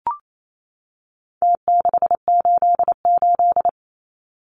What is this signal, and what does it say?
Morse code, heard as a keyed tone, about 28 wpm, 700 Hz. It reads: T688
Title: Numbers - Contest Style - 4-Digit 28wpm